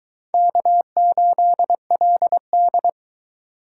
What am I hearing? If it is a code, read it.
Morse code, heard as a keyed tone, about 23 wpm, 700 Hz. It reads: K8LD